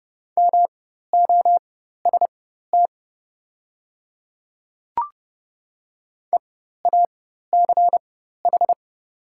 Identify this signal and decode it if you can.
Morse code, heard as a keyed tone, about 30 wpm, spaced as for 14 wpm, 690 Hz. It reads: MOST EACH
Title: 4x Character Spacing - Top 100 Words - 30wpm